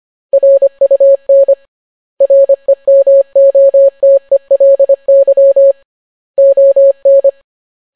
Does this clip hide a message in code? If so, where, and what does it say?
Yes: Morse code from 0.3 to 7.3 s, heard as a keyed tone, about 25 wpm, 550 Hz. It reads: RUN REMOTELY ON